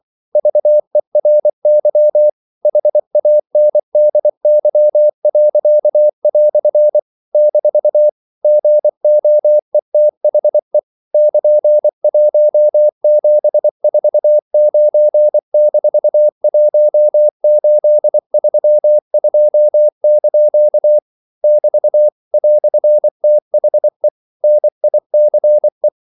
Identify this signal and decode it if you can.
Morse code, heard as a keyed tone, about 24 wpm, 600 Hz. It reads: VERY HANDY." - GOETHE <KN>1749-1832) <BT> "THE NICE
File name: CW / Practice / EnglishQuotations0324